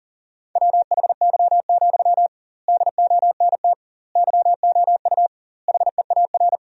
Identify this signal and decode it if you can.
Morse code, heard as a keyed tone, about 40 wpm, 700 Hz. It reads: WHY, DONT YOU HEAR
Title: 2x Word-Spacing - Sentences from Top 300 Words - 40wpm